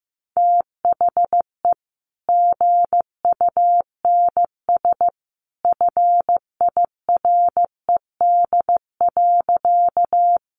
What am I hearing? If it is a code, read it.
Morse code, heard as a keyed tone, about 15 wpm, 700 Hz. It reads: THE GUNS FIRED.
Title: A Sound Of Thunder 15wpm